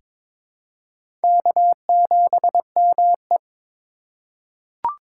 Morse code, heard as a keyed tone, about 21 wpm, 700 Hz. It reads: K7ME
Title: USA Call Signs - 4 Characters - 4x Character Spacing - 22wpm